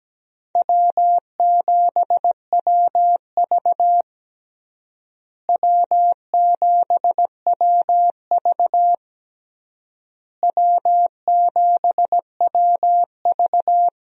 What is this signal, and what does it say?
Morse code, heard as a keyed tone, about 17 wpm, 700 Hz. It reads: W7WV W7WV W7WV
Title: USA Call Signs - 4 Characters - Repeated 3x - 17wpm